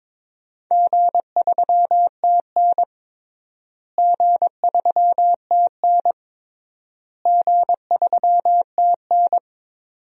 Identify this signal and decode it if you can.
Morse code, heard as a keyed tone, about 22 wpm, 700 Hz. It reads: G3TN G3TN G3TN
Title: DXCC-100 Call Signs - 4 Characters - Repeated 3x - 22wpm